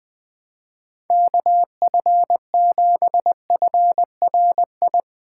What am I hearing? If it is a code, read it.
Morse code, heard as a keyed tone, about 21 wpm, 700 Hz. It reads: KF7FRI
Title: Speed Racing - USA 2x3 Call Signs 17wpm